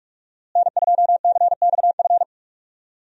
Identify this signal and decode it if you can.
Morse code, heard as a keyed tone, about 42 wpm, 700 Hz. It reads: N2CXF
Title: USA Call Signs - 5 Characters - Repeated 3x - 45wpm